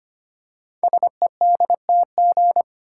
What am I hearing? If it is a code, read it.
Morse code, heard as a keyed tone, about 25 wpm, 700 Hz. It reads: SEDTG
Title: Code Groups - 5 Letter:Numbers - 25wpm